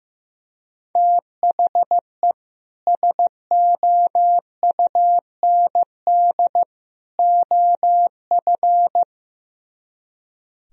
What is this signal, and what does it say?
Morse code, heard as a keyed tone, about 15 wpm, 700 Hz. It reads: THE SOUND OF
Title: Speed-Racing - Sets of 3 Words 15wpm